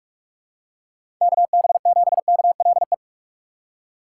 Morse code, heard as a keyed tone, about 46 wpm, 700 Hz. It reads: KB7KLE
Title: USA Call Signs - 6 Characters - 3x Character Spacing - 45wpm